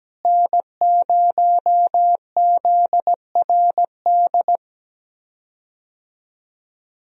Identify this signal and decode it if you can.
Morse code, heard as a keyed tone, about 17 wpm, 700 Hz. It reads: N0ZRD
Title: USA Call Signs - 5 Characters - Repeated 2x - 17wpm